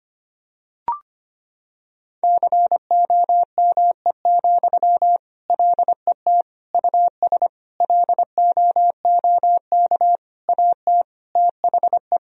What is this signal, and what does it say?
Morse code, heard as a keyed tone, about 25 wpm, 700 Hz. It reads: COME, LET US LOOK AT THE
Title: Sentences from Top 1000 Words - Review Part 3 - 25wpm